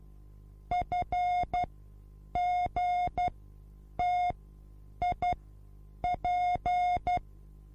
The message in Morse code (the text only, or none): FGTIP